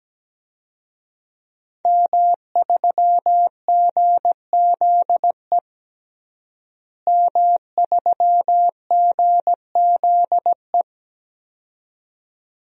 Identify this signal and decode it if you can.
Morse code, heard as a keyed tone, about 17 wpm, 700 Hz. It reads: M3GZE M3GZE
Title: DXCC-100 Call Signs - 5 Characters - Repeated 2x - 17wpm